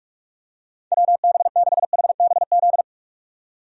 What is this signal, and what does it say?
Morse code, heard as a keyed tone, about 46 wpm, 700 Hz. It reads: WB6HBZ
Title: USA Call Signs - 6 Characters - 45wpm